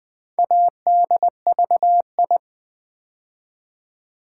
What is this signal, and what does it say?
Morse code, heard as a keyed tone, about 20 wpm, 700 Hz. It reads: ADVI
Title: Build A Word - 2 Letters At A Time - 6-Letter Words - 20wpm